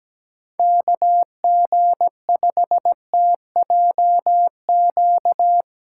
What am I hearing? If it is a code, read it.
Morse code, heard as a keyed tone, about 17 wpm, 700 Hz. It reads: KG5TJQ
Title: USA Callsign DE Callsign - 6 Characters - 17wpm